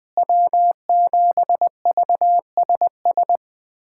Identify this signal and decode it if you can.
Morse code, heard as a keyed tone, about 20 wpm, 700 Hz. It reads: W7VSS
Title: USA Callsign DE Callsign - 5 Characters - 20wpm